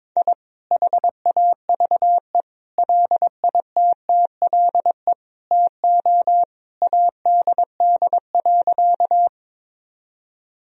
Morse code, heard as a keyed tone, about 22 wpm, 700 Hz. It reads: I HAVE LITTLE TO ADD.
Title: Sentences from Top 1000 Words - Review Part 1 - 22wpm